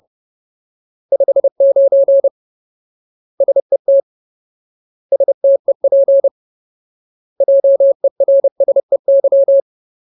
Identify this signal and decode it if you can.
Morse code, heard as a keyed tone, about 30 wpm, 550 Hz. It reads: 59 SET STEP JERSEY
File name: Ultimate Word List / 30WPM Ultimate